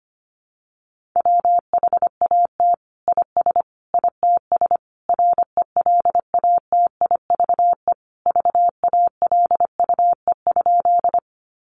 Morse code, heard as a keyed tone, about 25 wpm, 700 Hz. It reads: WHAT IS ITS RELATIVE VALUE?
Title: Sentences from Top 1400 Words 25wpm